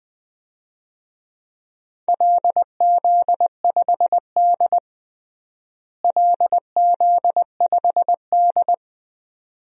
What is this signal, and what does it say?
Morse code, heard as a keyed tone, about 20 wpm, 700 Hz. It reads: LZ5D LZ5D
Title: DXCC-100 Call Signs - 4 Characters - Repeated 2x - 20wpm